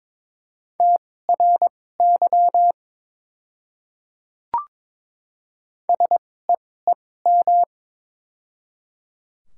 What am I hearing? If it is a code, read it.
Morse code, heard as a keyed tone, about 22 wpm, spaced as for 16 wpm, 700 Hz. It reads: TRY SEEM
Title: 2x Character Spacing - Top 300 Words - 22wpm